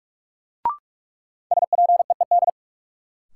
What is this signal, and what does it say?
Morse code, heard as a keyed tone, about 42 wpm, 700 Hz. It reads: SPEED
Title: Sentences from Top 800 Words - New Word Drill - 45wpm